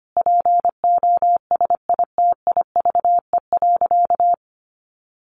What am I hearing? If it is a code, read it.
Morse code, heard as a keyed tone, about 25 wpm, 700 Hz. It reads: POSITIVE.
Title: Sentences from Top 1400 Words 25wpm